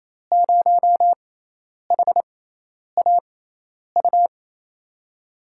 Morse code, heard as a keyed tone, about 27 wpm, spaced as for 10 wpm, 700 Hz. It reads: 0HAU
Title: USA Call Signs - 5 Characters - 6x Character Spacing - 28wpm